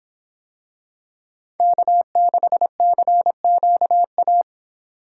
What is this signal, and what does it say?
Morse code, heard as a keyed tone, about 26 wpm, 700 Hz. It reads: K6CQA